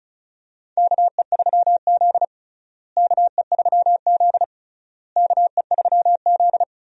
Morse code, heard as a keyed tone, about 36 wpm, 700 Hz. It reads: KE3Z KE3Z KE3Z
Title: USA Call Signs - 4 Characters - Repeated 3x - 35wpm